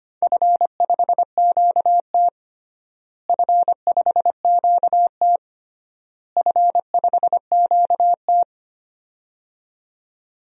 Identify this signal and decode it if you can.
Morse code, heard as a keyed tone, about 25 wpm, 700 Hz. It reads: F5QT F5QT F5QT
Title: DXCC-100 Call Signs - 4 Characters - Repeated 3x - 25wpm